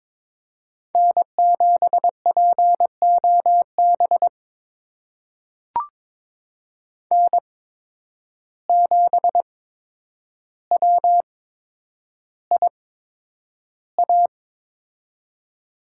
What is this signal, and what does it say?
Morse code, heard as a keyed tone, about 22 wpm, spaced as for 9 wpm, 700 Hz. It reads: N7POB N7WIA
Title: USA Call Signs - 5 Characters - 8x Character Spacing - 22wpm